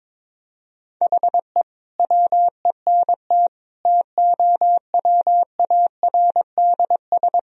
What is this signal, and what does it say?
Morse code, heard as a keyed tone, about 22 wpm, 700 Hz. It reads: HE WENT TOWARDS
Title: Sentences from Top 1000 Words - Review Part 2 - 22wpm